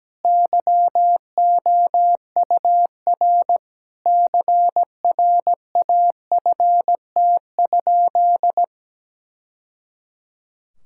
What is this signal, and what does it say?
Morse code, heard as a keyed tone, about 17 wpm, 700 Hz. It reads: YOUR CRAFT?
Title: Sentences from Top 1000 Words - Review Part 6 - 17wpm